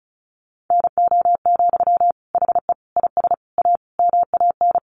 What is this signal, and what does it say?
Morse code, heard as a keyed tone, about 34 wpm, 700 Hz. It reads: NO, HE IS A MAN
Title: Sentences from Top 400 Words 35wpm